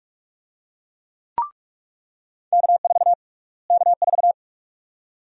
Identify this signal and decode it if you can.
Morse code, heard as a keyed tone, about 45 wpm, 700 Hz. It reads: K4 K4
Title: USA Call Sign Prefixes - Repeated 2x - 45wpm